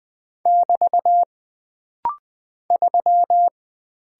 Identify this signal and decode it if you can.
Morse code, heard as a keyed tone, about 20 wpm, 700 Hz. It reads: <BT> 3
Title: Single Character (Letter:Number:Punctuation:Prosign) - Rapid-Fire - 20wpm